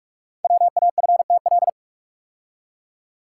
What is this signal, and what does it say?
Morse code, heard as a keyed tone, about 44 wpm, 700 Hz. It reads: WAFTL
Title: Code Groups - 5 Letters - 45wpm